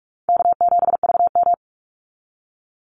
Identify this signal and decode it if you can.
Morse code, heard as a keyed tone, about 44 wpm, 700 Hz. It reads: K7VM
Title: Call Signs - 45wpm